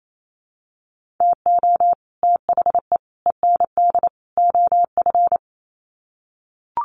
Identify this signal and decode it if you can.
Morse code, heard as a keyed tone, about 28 wpm, 700 Hz. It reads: TO THE END OF
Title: Sets of 4 Words 28wpm